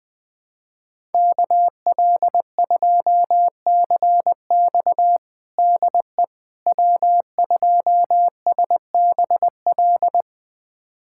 Morse code, heard as a keyed tone, about 20 wpm, 700 Hz. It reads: KL2CX DE W2SBL
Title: USA Callsign DE Callsign - 5 Characters - 20wpm